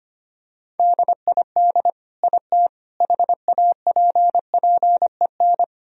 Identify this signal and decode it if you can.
Morse code, heard as a keyed tone, about 25 wpm, 700 Hz. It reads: DID IT HAPPEN
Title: Sentences from Top 1000 Words - Review Part 2 - 25wpm